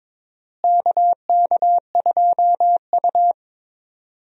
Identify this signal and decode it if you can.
Morse code, heard as a keyed tone, about 22 wpm, 700 Hz. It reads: KK2U
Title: USA Call Signs - 4 Characters - Repeated 2x - 22wpm